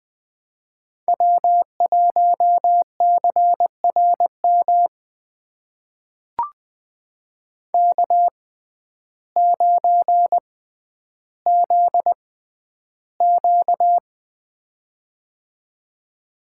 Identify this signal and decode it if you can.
Morse code, heard as a keyed tone, about 20 wpm, spaced as for 11 wpm, 700 Hz. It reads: W1CRM K9ZQ
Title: USA Call Signs - 6x Character Spacing - 20wpm